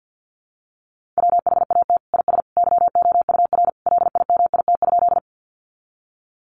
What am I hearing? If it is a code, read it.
Morse code, heard as a keyed tone, about 50 wpm, 700 Hz. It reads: WHAT IS YOUR LIMIT?
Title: Sentences from Top 1500 Words 50wpm